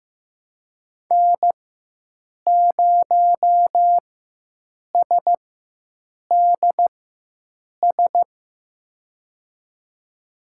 Morse code, heard as a keyed tone, about 15 wpm, spaced as for 7 wpm, 700 Hz. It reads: N0SDS
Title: USA Call Signs - 5 Characters - 4x Character Spacing - 15wpm